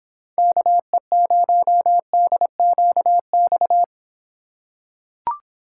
Morse code, heard as a keyed tone, about 26 wpm, 700 Hz. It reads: KE0DQX